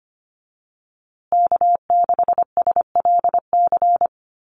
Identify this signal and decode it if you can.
Morse code, heard as a keyed tone, about 25 wpm, 700 Hz. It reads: K6SLC
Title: USA Call Signs Repeated 2x 25wpm